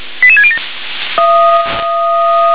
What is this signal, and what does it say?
Five-tone selective call (ZVEI), heard as quick runs of five tones, tones of about 70 ms. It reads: 88607